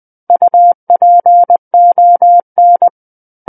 Morse code, heard as a keyed tone, about 20 wpm, 700 Hz. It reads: UPON
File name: 20 WPM